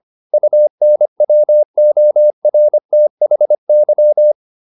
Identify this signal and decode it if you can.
Morse code, heard as a keyed tone, about 25 wpm, 600 Hz. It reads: UNWORTHY